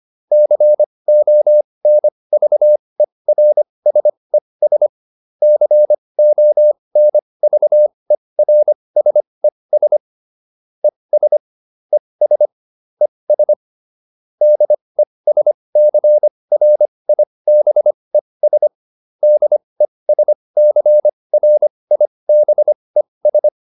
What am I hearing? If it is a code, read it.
Morse code, heard as a keyed tone, about 25 wpm, spaced as for 20 wpm, 600 Hz. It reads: CONVERSES CONVERSES ES ES ES DESCRIBES DESCRIBES
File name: ES8 20